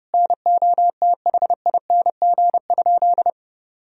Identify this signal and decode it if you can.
Morse code, heard as a keyed tone, about 30 wpm, 700 Hz. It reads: NOTHING?
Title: Sentences from Top 500 Words and Abbreviations 30wpm